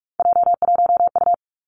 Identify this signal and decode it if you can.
Morse code, heard as a keyed tone, about 46 wpm, 700 Hz. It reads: J1U
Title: Three Challenging Letter-Number Combinations 45wpm